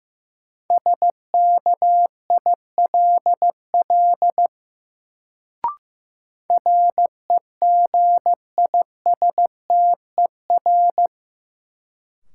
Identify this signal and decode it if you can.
Morse code, heard as a keyed tone, about 16 wpm, 700 Hz. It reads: SKILL REGISTER
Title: Sentences from Top 1200 Words - New Word Drill - 15wpm